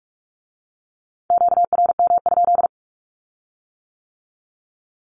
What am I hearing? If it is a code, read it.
Morse code, heard as a keyed tone, about 46 wpm, 700 Hz. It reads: QRM?
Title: Q-Codes with Spoken Meaning - 45wpm